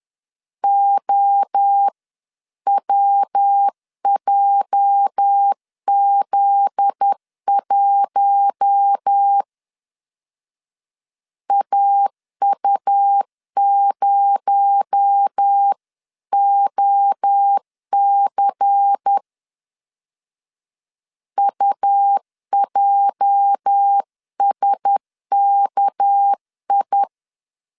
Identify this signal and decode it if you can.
Morse code, heard as a keyed tone, about 10 wpm, 800 Hz. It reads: OWJZ1 AU0OC UJSKI
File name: LEZIONI / 8 YJ45 / 8 YJ 10wpm 2651